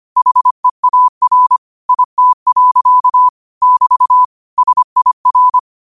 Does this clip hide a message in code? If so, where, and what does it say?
Yes: Morse code from 0.2 to 5.6 s, heard as a keyed tone, about 25 wpm, 1000 Hz. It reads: HEAR IT. <BT> SIR